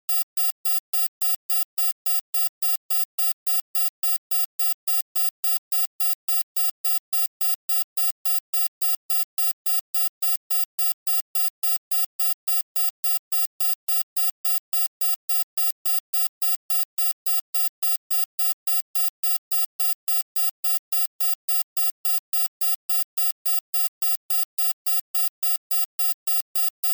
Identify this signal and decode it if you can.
There is no code.